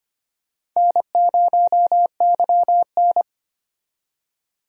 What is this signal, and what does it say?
Morse code, heard as a keyed tone, about 25 wpm, 700 Hz. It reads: N0YN
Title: USA Call Signs - 4 Characters - 25wpm